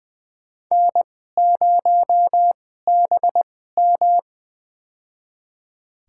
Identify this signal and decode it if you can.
Morse code, heard as a keyed tone, about 20 wpm, spaced as for 15 wpm, 700 Hz. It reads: N0BM
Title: USA Call Signs - 4 Characters - 2x Character Spacing - 20wpm